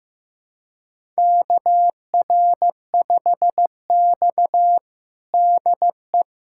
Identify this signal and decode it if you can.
Morse code, heard as a keyed tone, about 15 wpm, 700 Hz. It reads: KR5X DE
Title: USA Callsign DE Callsign - 4 Characters - 15wpm